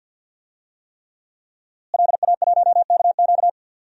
Morse code, heard as a keyed tone, about 46 wpm, 700 Hz. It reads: LA1KQ
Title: DXCC-100 Call Signs - 5 Characters - Repeated 3x - 50wpm